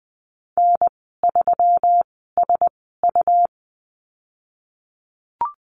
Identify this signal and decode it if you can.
Morse code, heard as a keyed tone, about 20 wpm, spaced as for 15 wpm, 700 Hz. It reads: N3SU
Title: 167 Lesson 27g - Number 7 - Call Signs 20wpm